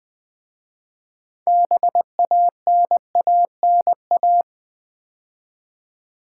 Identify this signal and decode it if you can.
Morse code, heard as a keyed tone, about 20 wpm, 700 Hz. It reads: BANANA
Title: 6 Letter Words - First 3 Letters Given - 20wpm